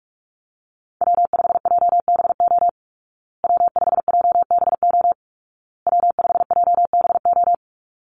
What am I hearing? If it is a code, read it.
Morse code, heard as a keyed tone, about 44 wpm, 700 Hz. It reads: W5JBO W5JBO W5JBO
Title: Call Signs Repeated 2x - 45wpm